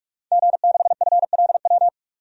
Morse code, heard as a keyed tone, about 46 wpm, 700 Hz. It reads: G6FLW
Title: DXCC-100 Call Signs - 5 Characters - 45wpm